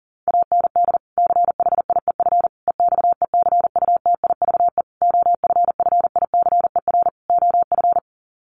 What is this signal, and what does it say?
Morse code, heard as a keyed tone, about 40 wpm, 700 Hz. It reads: AND CHIEF EXECUTIVE OFFICER OF
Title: Sets of 5 Words Encore2 40wpm